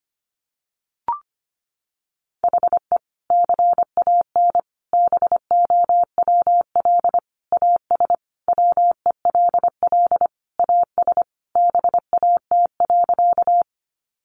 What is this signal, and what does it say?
Morse code, heard as a keyed tone, about 25 wpm, 700 Hz. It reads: HE CAN BOWL AS WELL AS BAT.
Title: Sentences from Top 1500 Words 25wpm